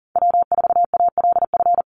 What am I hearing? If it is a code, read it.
Morse code, heard as a keyed tone, about 40 wpm, 700 Hz. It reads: W4ALF
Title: Call Signs - 40wpm